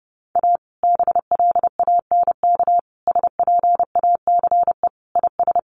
Morse code, heard as a keyed tone, about 30 wpm, 700 Hz. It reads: A BLANK SPACE IS
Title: Sentences from Top 1500 Words 30wpm